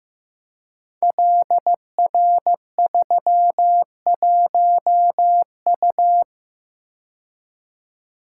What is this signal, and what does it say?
Morse code, heard as a keyed tone, about 15 wpm, 700 Hz. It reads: LR31U
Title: Code Groups - 5 Letter:Numbers - 15wpm